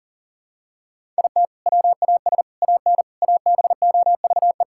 Morse code, heard as a keyed tone, about 40 wpm, 700 Hz. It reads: IT WAS AN ABOVE
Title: Sentences from Top 1000 Words - Review Part 6 - 40wpm